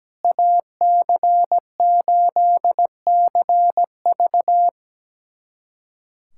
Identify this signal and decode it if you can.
Morse code, heard as a keyed tone, about 17 wpm, 700 Hz. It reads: AC8CV
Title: USA Callsign DE Callsign - 5 Characters - 17wpm